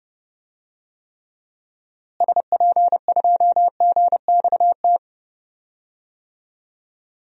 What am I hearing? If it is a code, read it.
Morse code, heard as a keyed tone, about 30 wpm, 700 Hz. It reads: SP2GXT
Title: DXCC-100 Call Signs - 6 Characters - Repeated 2x - 30wpm